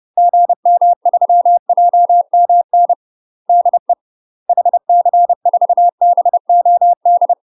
Morse code, heard as a keyed tone, about 30 wpm, 700 Hz. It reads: GM3JMN DE HC4BOD
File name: UK MASTER QSOs TRAINING / 30WPM / UK MASTER QSOs 0034